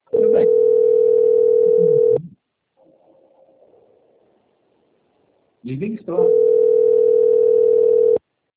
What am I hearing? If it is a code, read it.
Ringback tone (North America), 440 + 480 Hz, with speech.
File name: out- - - - - .735